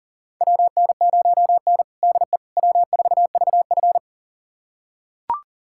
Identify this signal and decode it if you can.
Morse code, heard as a keyed tone, about 40 wpm, 700 Hz. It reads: WN0N DE W4VF
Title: USA Callsign DE Callsign - 4 Characters - 40wpm